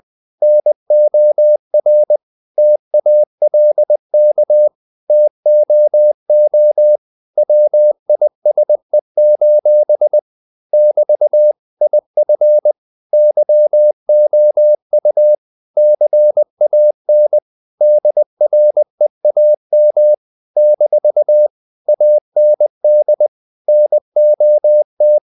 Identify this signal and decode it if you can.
Morse code, heard as a keyed tone, about 20 wpm, 600 Hz. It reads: NOR TALK TOO WISE: <BT> IF YOU CAN DREAM - AND NOT
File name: CW / Practice / If20